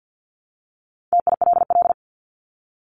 Morse code, heard as a keyed tone, about 49 wpm, 700 Hz. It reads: TILL
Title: Top 1751-2000 Words 50wpm